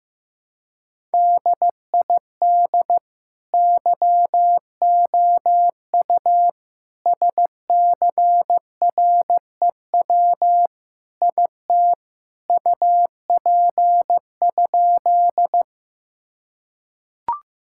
Morse code, heard as a keyed tone, about 15 wpm, 700 Hz. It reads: DID YOU SCREW IT UP?
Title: Sentences from Top 1600 Words 15wpm